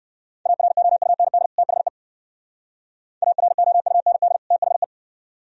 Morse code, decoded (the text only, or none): AROUND THE AROUND THE